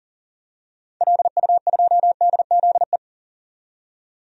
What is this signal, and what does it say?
Morse code, heard as a keyed tone, about 40 wpm, 700 Hz. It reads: LU2DZE
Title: DXCC-100 Call Signs - 6 Characters - 40wpm